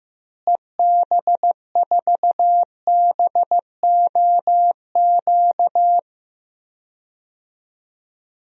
Morse code, heard as a keyed tone, about 15 wpm, 700 Hz. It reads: EB4BOQ
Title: DXCC-100 Call Signs - 6 Characters - Repeated 3x - 15wpm